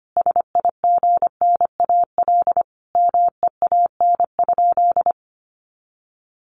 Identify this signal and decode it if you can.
Morse code, heard as a keyed tone, about 25 wpm, 700 Hz. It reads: SIGNAL MEAN?
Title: Sentences from Top 1400 Words 25wpm